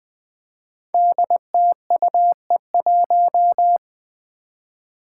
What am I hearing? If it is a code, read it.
Morse code, heard as a keyed tone, about 20 wpm, 700 Hz. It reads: DTUE1
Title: Five Challenging Letter-Number Combinations 20wpm